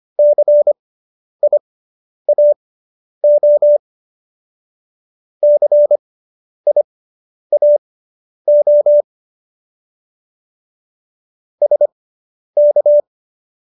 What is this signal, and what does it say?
Morse code, heard as a keyed tone, about 25 wpm, spaced as for 10 wpm, 590 Hz. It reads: CIAO CIAO SK